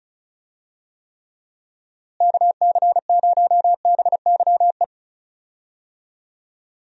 Morse code, heard as a keyed tone, about 34 wpm, 700 Hz. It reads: KC0BYE
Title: USA Call Signs - 6 Characters - Repeated 2x - 35wpm